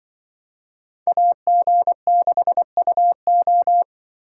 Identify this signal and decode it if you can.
Morse code, heard as a keyed tone, about 24 wpm, 700 Hz. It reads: AG6UO
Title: Speed Racing - USA 2x2 Call Signs 20wpm